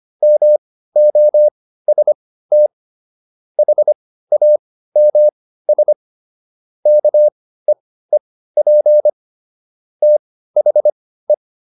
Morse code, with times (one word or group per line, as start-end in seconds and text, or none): 0.2-2.7 MOST
3.6-5.9 HAMS
6.9-9.1 KEEP
10.0-11.3 THE